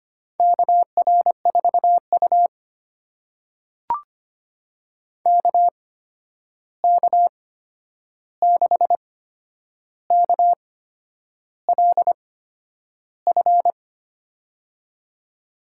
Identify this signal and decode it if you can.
Morse code, heard as a keyed tone, about 25 wpm, spaced as for 9 wpm, 700 Hz. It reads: KR4U KK6KLF
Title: USA Call Signs - 8x Character Spacing - 25wpm